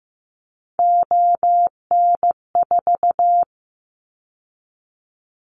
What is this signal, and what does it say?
Morse code, heard as a keyed tone, about 15 wpm, 700 Hz. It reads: ON4
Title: Callsign Prefixes for Top 100 DXCC - 15wpm